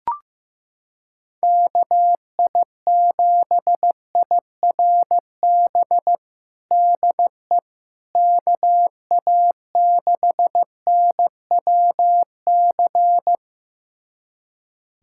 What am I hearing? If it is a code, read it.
Morse code, heard as a keyed tone, about 16 wpm, 700 Hz. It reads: KI7IRB DE KA6NWC
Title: USA Callsign DE Callsign - 6 Characters - 15wpm